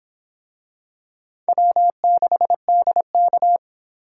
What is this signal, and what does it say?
Morse code, heard as a keyed tone, about 24 wpm, 700 Hz. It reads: W6DK